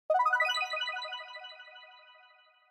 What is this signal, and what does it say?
There is no code.